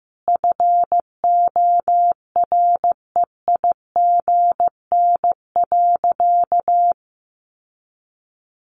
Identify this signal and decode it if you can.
Morse code, heard as a keyed tone, about 14 wpm, 700 Hz. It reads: FOREIGN.
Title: Sentences from Top 1400 Words 15wpm